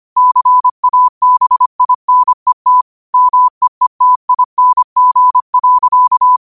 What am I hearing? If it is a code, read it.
Morse code, heard as a keyed tone, about 25 wpm, 1000 Hz. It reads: CABINET MEETING.